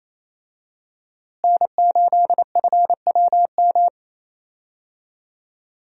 Morse code, heard as a keyed tone, about 28 wpm, 700 Hz. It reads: N8FWM
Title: USA Call Signs - 5 Characters - Repeated 2x - 28wpm